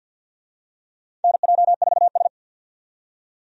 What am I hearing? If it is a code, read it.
Morse code, heard as a keyed tone, about 52 wpm, 700 Hz. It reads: NJ4S
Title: USA Call Signs - 4 Characters - 50wpm